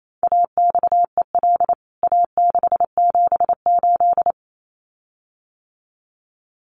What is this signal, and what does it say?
Morse code, heard as a keyed tone, about 28 wpm, 700 Hz. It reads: AXEL A678
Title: CWT Exchange 28wpm